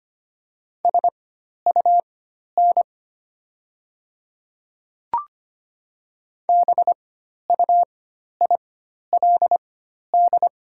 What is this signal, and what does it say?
Morse code, heard as a keyed tone, about 25 wpm, spaced as for 12 wpm, 700 Hz. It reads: SUN BUILD
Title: 4x Character Spacing - Top 300 Words - 25wpm